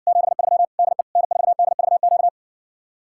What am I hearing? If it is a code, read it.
Morse code, heard as a keyed tone, about 60 wpm, 700 Hz. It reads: 73 DE N4DVQ